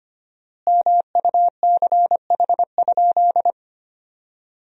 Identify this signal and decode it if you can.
Morse code, heard as a keyed tone, about 25 wpm, 700 Hz. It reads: MUCH?
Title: 4x Word-Spacing - Sentences from Top 200 Words - 25wpm